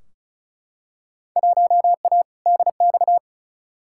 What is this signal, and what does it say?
Morse code, heard as a keyed tone, about 36 wpm, 700 Hz. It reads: 1A DX
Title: Field Day Exchanges Encore - 35wpm